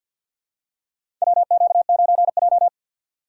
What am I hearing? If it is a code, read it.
Morse code, heard as a keyed tone, about 54 wpm, 700 Hz. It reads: WQ9J
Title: USA Call Signs - 4 Characters - 6x Character Spacing - 50wpm